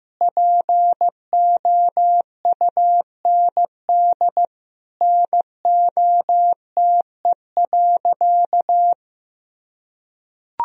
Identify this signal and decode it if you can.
Morse code, heard as a keyed tone, about 16 wpm, 700 Hz. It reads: POUND NOTE.
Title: Sentences from Top 1000 Words - Review Part 2 - 15wpm